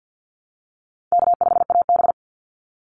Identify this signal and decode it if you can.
Morse code, heard as a keyed tone, about 50 wpm, 700 Hz. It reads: K5AB
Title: Easy Call Signs Repeated 2x - 50wpm